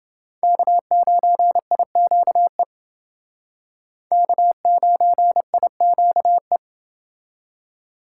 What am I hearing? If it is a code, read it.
Morse code, heard as a keyed tone, about 30 wpm, 700 Hz. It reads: K9IQE K9IQE